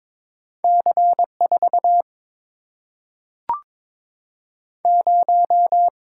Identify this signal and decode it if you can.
Morse code, heard as a keyed tone, about 22 wpm, 700 Hz. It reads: C 4 0
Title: Single-Letter Single-Number 22wpm